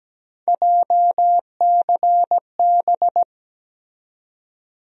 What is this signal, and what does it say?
Morse code, heard as a keyed tone, about 17 wpm, 700 Hz. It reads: JCB
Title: USA Call Sign Postfixes - 3 Letters 17wpm